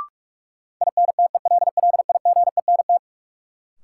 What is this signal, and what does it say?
Morse code, heard as a keyed tone, about 46 wpm, 700 Hz. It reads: INTELLIGENT
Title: Sentences from Top 1000 Words - Review Part 7 - New Word Drill - 45wpm